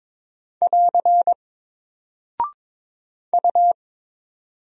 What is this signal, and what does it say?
Morse code, heard as a keyed tone, about 22 wpm, 700 Hz. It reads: <AR> U
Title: Single Character (Letter:Number:Punctuation:Prosign) 22wpm